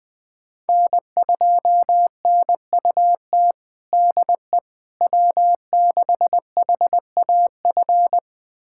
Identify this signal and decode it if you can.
Morse code, heard as a keyed tone, about 20 wpm, 700 Hz. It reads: N2NUT DE W6HAF
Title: USA Callsign DE Callsign - 5 Characters - 20wpm